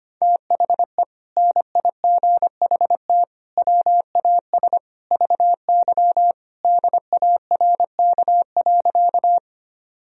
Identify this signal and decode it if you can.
Morse code, heard as a keyed tone, about 25 wpm, 700 Hz. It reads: THE NIGHT WAS VY DARK.
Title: Sentences from Top 500 Words and Abbreviations 25wpm